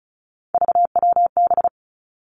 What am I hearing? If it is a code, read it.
Morse code, heard as a keyed tone, about 36 wpm, 700 Hz. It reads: VWB
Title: Three Challenging Letter-Number Combinations 35wpm